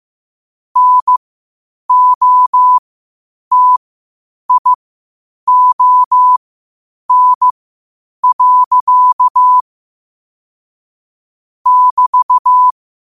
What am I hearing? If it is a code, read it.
Morse code, heard as a keyed tone, about 15 wpm, spaced as for 8 wpm, 1000 Hz. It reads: NOTION. <BT>
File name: qotdfiles / 8wpm